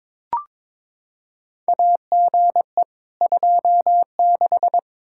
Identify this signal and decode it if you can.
Morse code, heard as a keyed tone, about 22 wpm, 700 Hz. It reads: AGE 26
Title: Practice QSO - Exchange 3 - More Info - 22wpm